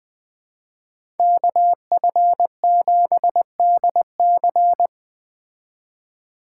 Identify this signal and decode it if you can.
Morse code, heard as a keyed tone, about 20 wpm, 700 Hz. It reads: KF7DC
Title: Speed Racing - USA 2x2 Call Signs 17wpm